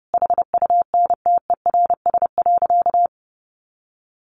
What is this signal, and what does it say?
Morse code, heard as a keyed tone, about 28 wpm, 700 Hz. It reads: HUNTERS.
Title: A Sound Of Thunder 30wpm